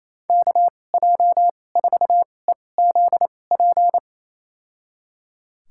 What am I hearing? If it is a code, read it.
Morse code, heard as a keyed tone, about 28 wpm, spaced as for 20 wpm, 700 Hz. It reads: KJ4EZP
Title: USA Call Signs - 6 Characters - 2x Character Spacing - 28wpm